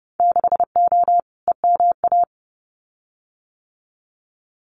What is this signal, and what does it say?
Morse code, heard as a keyed tone, about 30 wpm, 700 Hz. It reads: 6O EMA
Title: Winter Field Day Exchanges Encore 30wpm